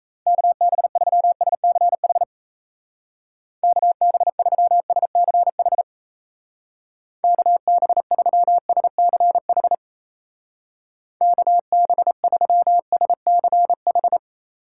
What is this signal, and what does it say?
Morse code, heard as a keyed tone, about 34 wpm, 700 Hz. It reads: KB3SCH KB3SCH KB3SCH KB3SCH
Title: Speed Racing - USA 2x3 Call Signs 28wpm